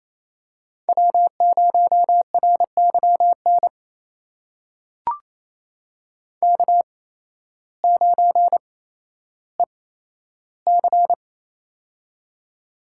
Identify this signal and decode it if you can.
Morse code, heard as a keyed tone, about 28 wpm, spaced as for 13 wpm, 700 Hz. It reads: W0RYN K9EC
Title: USA Call Signs - 8x Character Spacing - 28wpm